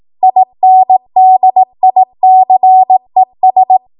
Morse code, heard as a keyed tone, about 18 wpm, 750 Hz. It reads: INDICES
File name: Archive / 18 WPM / 18WPM